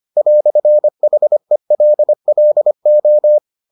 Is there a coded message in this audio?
Yes: Morse code from 0.2 to 3.4 s, heard as a keyed tone, about 25 wpm, 600 Hz. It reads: "HELLO